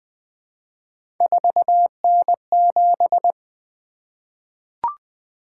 Morse code, heard as a keyed tone, about 20 wpm, 700 Hz. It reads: 4N7
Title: Three Letter-Number Combinations - 20wpm